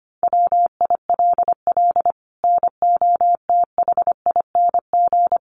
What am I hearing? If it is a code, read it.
Morse code, heard as a keyed tone, about 25 wpm, 700 Hz. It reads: WILL NOTHING
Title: Sentences from Top 1200 Words 25wpm